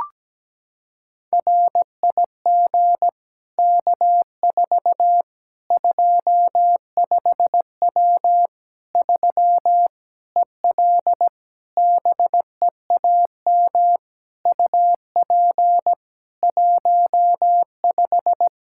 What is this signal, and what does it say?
Morse code, heard as a keyed tone, about 17 wpm, 700 Hz. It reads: RIG K4 25W 3 EL BEAM UP 15
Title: Practice QSO - Exchange 2 - Info - 17wpm